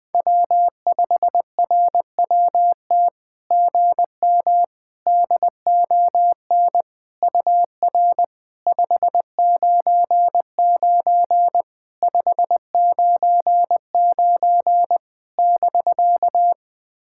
Morse code, heard as a keyed tone, about 20 wpm, 700 Hz. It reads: W5RWT GM DON UR 599 599 <BK>
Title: SOTA QSO Practice 20wpm